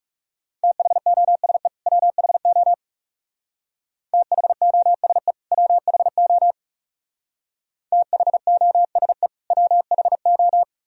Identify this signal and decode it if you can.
Morse code, heard as a keyed tone, about 36 wpm, 700 Hz. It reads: THOSE WHO THOSE WHO THOSE WHO
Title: Speed-Racing - Sets of 2 Words 30wpm